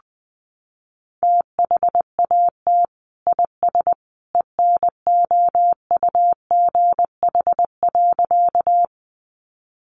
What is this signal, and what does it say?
Morse code, heard as a keyed tone, about 20 wpm, 700 Hz. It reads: THAT IS ENOUGH.
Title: Sentences from Top 400 Words 20wpm